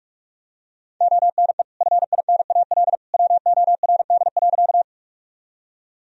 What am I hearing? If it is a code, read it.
Morse code, heard as a keyed tone, about 47 wpm, 700 Hz. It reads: ONE FINAL WORD.
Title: Sentences from Top 1000 Words - Review Part 2 - 45wpm